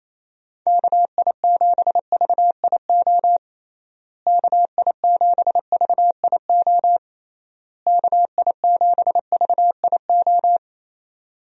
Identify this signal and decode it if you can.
Morse code, heard as a keyed tone, about 28 wpm, 700 Hz. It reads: KI7VIO KI7VIO KI7VIO
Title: USA Call Signs - 6 Characters - Repeated 3x - 28wpm